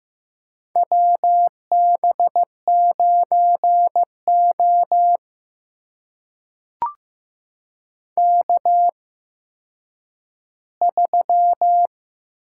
Morse code, heard as a keyed tone, about 15 wpm, spaced as for 9 wpm, 700 Hz. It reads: WB9O K3
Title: USA Call Signs - 8x Character Spacing - 15wpm